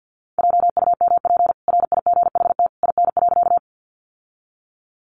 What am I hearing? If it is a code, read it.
Morse code, heard as a keyed tone, about 50 wpm, 700 Hz. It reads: JUMP RIGHT IN.
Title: Sentences from Top 1300 Words 50wpm